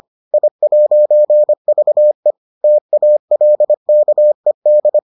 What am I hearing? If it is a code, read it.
Morse code, heard as a keyed tone, about 25 wpm, 600 Hz. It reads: I'VE TALKED